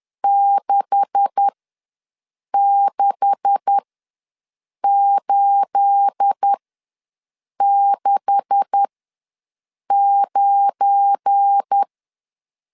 Morse code, heard as a keyed tone, about 10 wpm, spaced as for 6 wpm, 800 Hz. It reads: 66869
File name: LEZIONI / 9 286 ws 10wpm 4032